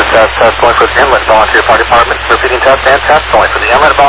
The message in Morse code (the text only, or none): none